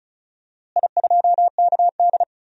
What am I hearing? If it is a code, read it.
Morse code, heard as a keyed tone, about 36 wpm, 700 Hz. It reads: I2KD
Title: DXCC-100 Call Signs - 4 Characters - Repeated 2x - 35wpm